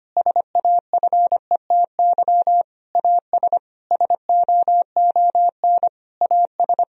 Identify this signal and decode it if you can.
Morse code, heard as a keyed tone, about 25 wpm, 700 Hz. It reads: SAFETY AS SOON AS
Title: Sentences from Top 1000 Words - Review Part 4 - 25wpm